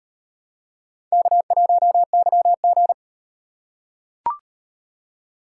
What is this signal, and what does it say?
Morse code, heard as a keyed tone, about 38 wpm, 700 Hz. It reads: K1YG